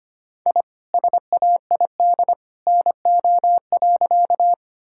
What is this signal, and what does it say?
Morse code, heard as a keyed tone, about 25 wpm, 700 Hz. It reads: I SAID NO.
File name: practice / 25wpm / Sentences with 2-Word Phrases - 25wpm